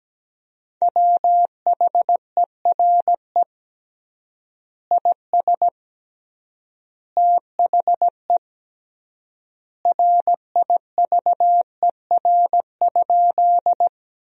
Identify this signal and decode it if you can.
Morse code, heard as a keyed tone, about 17 wpm, 700 Hz. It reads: WHERE IS THE RIVER?
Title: 3x Word-Spacing - Sentences from Top 300 Words - 17wpm